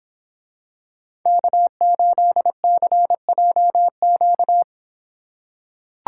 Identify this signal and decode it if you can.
Morse code, heard as a keyed tone, about 26 wpm, 700 Hz. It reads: K8CJQ